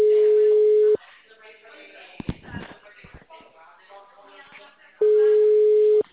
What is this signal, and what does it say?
Ringback tone (Europe, 425 Hz).
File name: out- -2008- - - .129